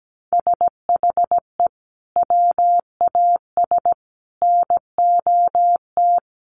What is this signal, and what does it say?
Morse code, heard as a keyed tone, about 17 wpm, 700 Hz. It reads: SHE WAS NOT
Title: Sets of 5 Words Encore2 17wpm